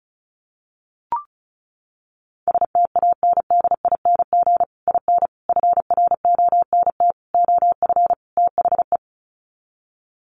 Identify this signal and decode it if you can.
Morse code, heard as a keyed tone, about 34 wpm, 700 Hz. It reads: STANDING IN FRONT OF THE
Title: Sets of 5 Words Encore2 35wpm